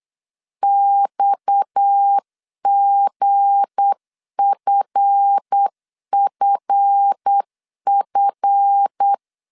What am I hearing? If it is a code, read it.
Morse code, heard as a keyed tone, about 8 wpm, 800 Hz. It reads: XGFFF
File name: LEZIONI / 7 GF9Z / 7 GF 8wpm 3108